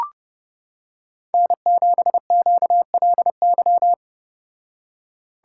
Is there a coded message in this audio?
Yes: Morse code from 1.3 to 3.9 s, heard as a keyed tone, about 29 wpm, 700 Hz. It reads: N7QLY